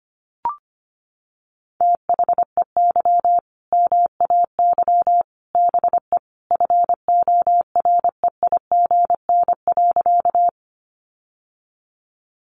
Morse code, heard as a keyed tone, about 25 wpm, 700 Hz. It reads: THEY MAY BE FOREIGN.
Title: Sentences from Top 1400 Words 25wpm